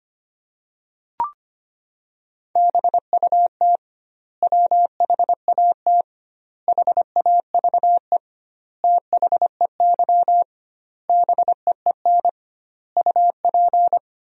Morse code, heard as a keyed tone, about 25 wpm, 700 Hz. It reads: BUT WHAT HAVE THEY BEEN UP
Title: 2x Word-Spacing - Sentences from Top 100 Words - 25wpm